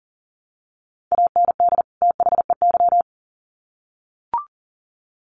Morse code, heard as a keyed tone, about 40 wpm, 700 Hz. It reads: AND THEY
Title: Sets of 2 Words 40wpm